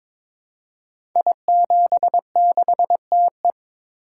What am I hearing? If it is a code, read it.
Morse code, heard as a keyed tone, about 22 wpm, 700 Hz. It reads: I76TE
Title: Code Groups - 5 Letter:Numbers - 22wpm